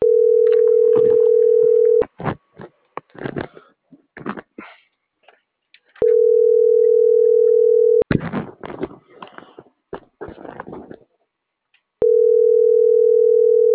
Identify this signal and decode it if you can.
Ringback tone (North America), 440 + 480 Hz.